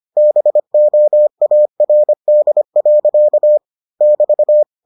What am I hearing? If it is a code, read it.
Morse code, heard as a keyed tone, about 25 wpm, 600 Hz. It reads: BOARD. <BT>